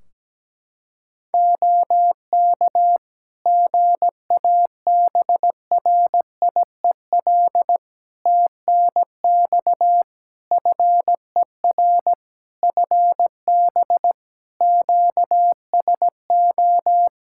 Morse code, heard as a keyed tone, about 17 wpm, 700 Hz. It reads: OK GABRIEL TNX FER FB QSO
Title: Practice QSO - Exchange 4 - Ending - 17wpm